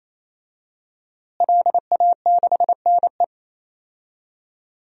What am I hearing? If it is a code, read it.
Morse code, heard as a keyed tone, about 28 wpm, 700 Hz. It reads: LA6NE
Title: DXCC-100 Call Signs - 5 Characters - Repeated 3x - 28wpm